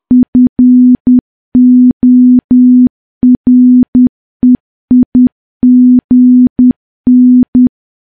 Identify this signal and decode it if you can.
Morse code, heard as a keyed tone, about 10 wpm, 250 Hz. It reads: FOREIGN
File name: morse / morse-training-10WPM